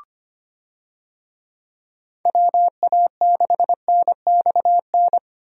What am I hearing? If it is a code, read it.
Morse code, heard as a keyed tone, about 25 wpm, 700 Hz. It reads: WA6NXN
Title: USA Call Signs - 6 Characters - Repeated 2x - 25wpm